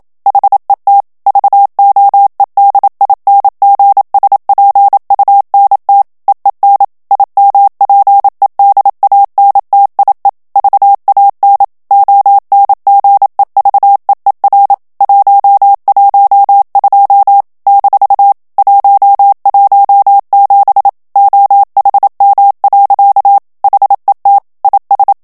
Morse code, heard as a keyed tone, about 27 wpm, 800 Hz. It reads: HET VOEDINGSPUNT EEN IMPEDANTIE VAN ONGEVEER 112 - 117 OHM. HET IS
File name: HOLLANDAIS texte clair 1 vitesse 26 mots minute